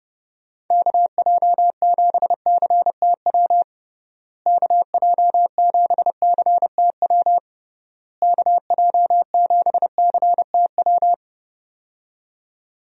Morse code, heard as a keyed tone, about 32 wpm, 700 Hz. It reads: KJ7CTW KJ7CTW KJ7CTW
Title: USA Call Signs - 6 Characters - Repeated 3x - 30wpm